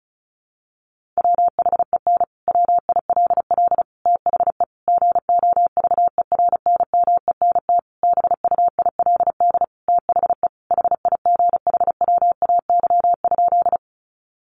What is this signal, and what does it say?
Morse code, heard as a keyed tone, about 34 wpm, 700 Hz. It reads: WHEN WILL THE GOVERNMENT BUILD THE HIGHWAY?
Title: Sentences from Top 1400 Words 35wpm